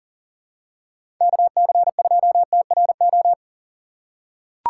Morse code, heard as a keyed tone, about 40 wpm, 700 Hz. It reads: KC2TRO
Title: USA Call Signs - 6 Characters - 3x Character Spacing - 40wpm